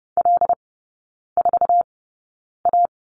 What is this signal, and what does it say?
Morse code, heard as a keyed tone, about 30 wpm, spaced as for 9 wpm, 700 Hz. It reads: L4A
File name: 081 Lesson 14e - Number 2 - Three Characters 30wpm